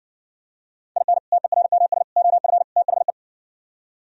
Morse code, heard as a keyed tone, about 60 wpm, 700 Hz. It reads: IN TERMS OF THE